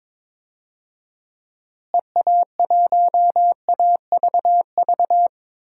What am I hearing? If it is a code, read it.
Morse code, heard as a keyed tone, about 22 wpm, 700 Hz. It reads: EA1AVV
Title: DXCC-100 Call Signs - 6 Characters - Repeated 3x - 22wpm